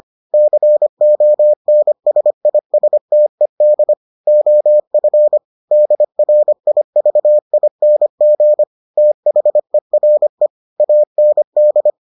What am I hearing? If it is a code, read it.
Morse code, heard as a keyed tone, about 25 wpm, 600 Hz. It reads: CONSISTED OF DRIVING THERE AND